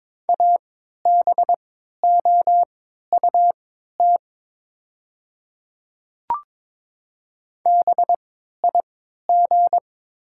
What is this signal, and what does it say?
Morse code, heard as a keyed tone, about 22 wpm, spaced as for 12 wpm, 690 Hz. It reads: ABOUT BIG
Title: 3x Character Spacing - Top 200 Words - 22wpm